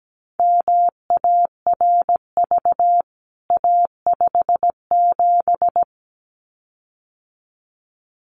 Morse code, heard as a keyed tone, about 17 wpm, 700 Hz. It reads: MARV A57
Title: CWT Exchange 17wpm